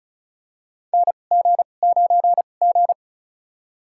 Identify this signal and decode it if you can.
Morse code, heard as a keyed tone, about 34 wpm, spaced as for 25 wpm, 700 Hz. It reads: NG9G
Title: USA Call Signs - 2x Character Spacing - 35wpm